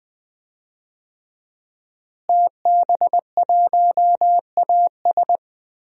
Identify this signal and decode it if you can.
Morse code, heard as a keyed tone, about 20 wpm, 700 Hz. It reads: TB1AS
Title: DXCC-100 Call Signs - 5 Characters - Repeated 2x - 20wpm